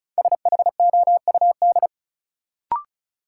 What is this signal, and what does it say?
Morse code, heard as a keyed tone, about 36 wpm, 700 Hz. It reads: SHOUD
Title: Sentences from Top 1000 Words - Review Part 4 - New Word Drill - 35wpm